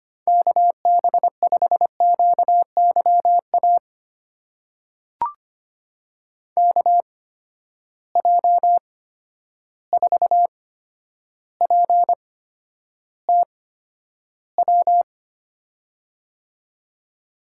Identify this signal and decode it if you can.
Morse code, heard as a keyed tone, about 25 wpm, spaced as for 11 wpm, 700 Hz. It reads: KB5QYA KJ4PTW
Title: USA Call Signs - 6 Characters - 8x Character Spacing - 25wpm